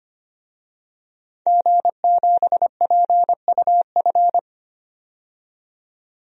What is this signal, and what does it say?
Morse code, heard as a keyed tone, about 25 wpm, 700 Hz. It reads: G7PUF
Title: DXCC-100 Call Signs - 5 Characters - 25wpm